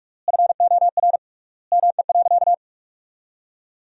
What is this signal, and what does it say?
Morse code, heard as a keyed tone, about 42 wpm, 700 Hz. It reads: FOR ME.
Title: 3x Word-Spacing - Sentences from Top 300 Words - 45wpm